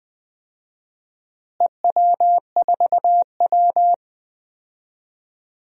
Morse code, heard as a keyed tone, about 20 wpm, 700 Hz. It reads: EW4W
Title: DXCC-100 Call Signs - 4 Characters - Repeated 3x - 20wpm